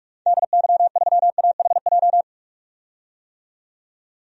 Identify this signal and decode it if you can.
Morse code, heard as a keyed tone, about 46 wpm, 690 Hz. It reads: DY3AHJ
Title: DXCC-100 Call Signs - 6 Characters - 45wpm